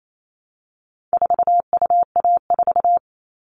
Morse code, heard as a keyed tone, about 28 wpm, 700 Hz. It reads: 4UA4
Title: All Cut Numbers - 4-Digit 28wpm